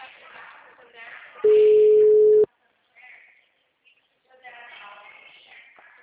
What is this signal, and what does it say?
Ringback tone (Europe, 425 Hz), with speech.